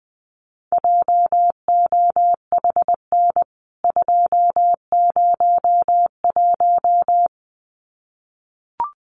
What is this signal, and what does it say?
Morse code, heard as a keyed tone, about 20 wpm, 700 Hz. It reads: JOHN 201
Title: CWT Exchange 20wpm